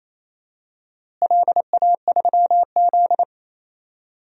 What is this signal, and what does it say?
Morse code, heard as a keyed tone, about 28 wpm, 700 Hz. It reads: LA3Z
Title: DXCC-100 Call Signs - 4 Characters - Repeated 3x - 28wpm